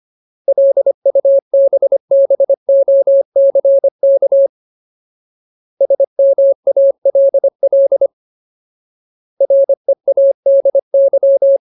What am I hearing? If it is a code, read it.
Morse code, heard as a keyed tone, about 25 wpm, 550 Hz. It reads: LUBBOCK SMALL READY
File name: Ultimate Word List / 25WPM Ultimate